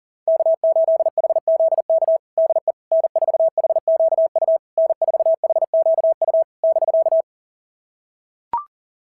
Morse code, heard as a keyed tone, about 37 wpm, 690 Hz. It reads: K8HZK DE N4HQU N4HQU <BK>
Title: SOTA QSO Practice 40wpm